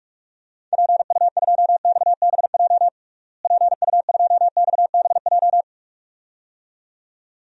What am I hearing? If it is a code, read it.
Morse code, heard as a keyed tone, about 46 wpm, 700 Hz. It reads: PU2XBJ PU2XBJ
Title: DXCC-100 Call Signs - 6 Characters - Repeated 2x - 45wpm